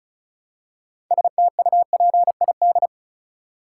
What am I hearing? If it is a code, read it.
Morse code, heard as a keyed tone, about 36 wpm, 700 Hz. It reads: STUPID
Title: Six Letter Words - 35wpm